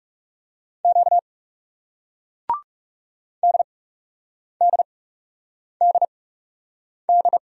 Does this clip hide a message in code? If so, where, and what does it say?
Yes: Morse code from 0.8 to 7.4 s, heard as a keyed tone, about 40 wpm, spaced as for 8 wpm, 700 Hz. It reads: Q DDDD